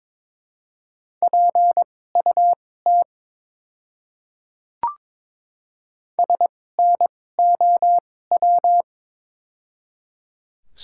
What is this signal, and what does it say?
Morse code, heard as a keyed tone, about 23 wpm, spaced as for 16 wpm, 700 Hz. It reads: PUT SNOW
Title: 2x Character Spacing - Top 500 Words - 22wpm